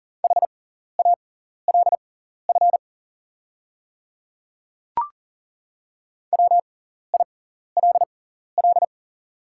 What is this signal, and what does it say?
Morse code, heard as a keyed tone, about 40 wpm, spaced as for 14 wpm, 690 Hz. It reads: HALF WILL
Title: 6x Character Spacing - Top 500 Words - 40wpm